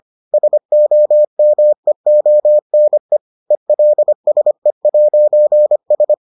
Morse code, heard as a keyed tone, about 25 wpm, 600 Hz. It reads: SOMEONE ELSE'S